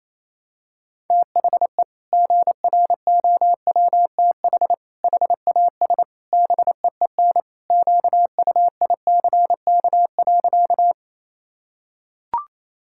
Morse code, heard as a keyed tone, about 28 wpm, 700 Hz. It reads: THE GROWTH HAS BEEN QUICK.
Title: Sentences from Top 1000 Words - Review Part 5 - 28wpm